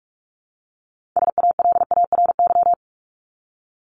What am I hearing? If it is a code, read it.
Morse code, heard as a keyed tone, about 44 wpm, 700 Hz. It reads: SALARY
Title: Top 1751-2000 Words 45wpm